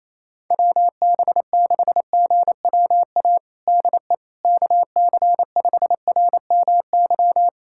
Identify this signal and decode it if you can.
Morse code, heard as a keyed tone, about 28 wpm, 700 Hz. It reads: WB6GWA DE KC5RMY
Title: USA Callsign DE Callsign 28wpm